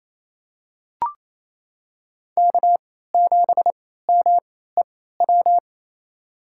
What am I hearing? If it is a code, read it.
Morse code, heard as a keyed tone, about 28 wpm, spaced as for 16 wpm, 690 Hz. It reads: K7MEW
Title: USA Call Signs - 3x Character Spacing - 28wpm